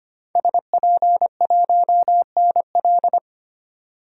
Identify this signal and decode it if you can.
Morse code, heard as a keyed tone, about 25 wpm, 700 Hz. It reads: SP1NL
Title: DXCC-100 Call Signs - 5 Characters - 25wpm